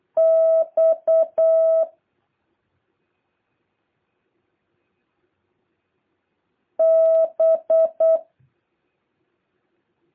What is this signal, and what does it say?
Morse code, heard as a keyed tone, about 8 wpm, 650 Hz. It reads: X B
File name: jh8cbh / 2023cw